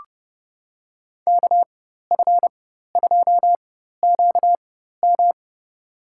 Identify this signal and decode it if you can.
Morse code, heard as a keyed tone, about 30 wpm, spaced as for 14 wpm, 700 Hz. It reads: KF2QM
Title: USA Call Signs - 5 Characters - 4x Character Spacing - 30wpm